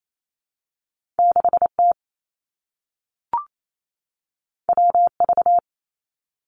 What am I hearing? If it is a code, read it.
Morse code, heard as a keyed tone, about 28 wpm, 700 Hz. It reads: 6 T W V
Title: Two Challenging Letter-Number Combinations 28wpm